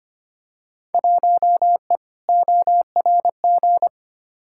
Morse code, heard as a keyed tone, about 25 wpm, 700 Hz. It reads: 1E ORG
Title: Field Day Exchanges - 25wpm